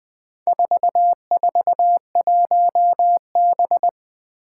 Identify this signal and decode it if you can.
Morse code, heard as a keyed tone, about 20 wpm, 700 Hz. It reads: 441B
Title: Four Challenging Letter-Number Combinations 20wpm